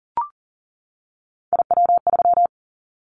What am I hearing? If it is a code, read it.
Morse code, heard as a keyed tone, about 40 wpm, 700 Hz. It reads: IW3
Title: Callsign Prefixes for Top 25 DXCC - 40wpm